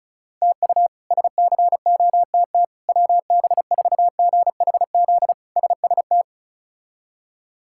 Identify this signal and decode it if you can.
Morse code, heard as a keyed tone, about 34 wpm, 700 Hz. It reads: TU SCOTT WB4GHZ SST
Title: K1USN SST QSO Practice - 35wpm